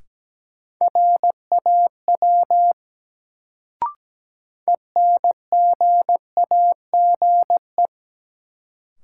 Morse code, heard as a keyed tone, about 17 wpm, 700 Hz. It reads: RAW ENGAGE
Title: Sentences from Top 1500 Words - New Word Drill Alt - 17wpm